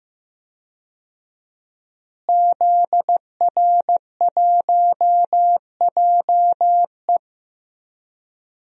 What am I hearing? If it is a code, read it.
Morse code, heard as a keyed tone, about 15 wpm, 700 Hz. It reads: ZR1JE
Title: DXCC-100 Call Signs - 5 Characters - Repeated 3x - 15wpm